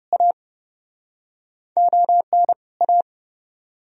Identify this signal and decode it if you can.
Morse code, heard as a keyed tone, about 29 wpm, 700 Hz. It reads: A ON A